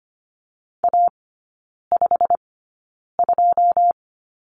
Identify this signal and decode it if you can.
Morse code, heard as a keyed tone, about 25 wpm, spaced as for 9 wpm, 700 Hz. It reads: A52
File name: course / 25:10wpm / 093 Lesson 16e - Letter c - Three Characters 25wpm